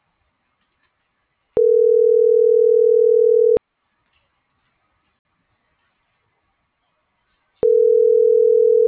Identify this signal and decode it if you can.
Ringback tone (North America), 440 + 480 Hz.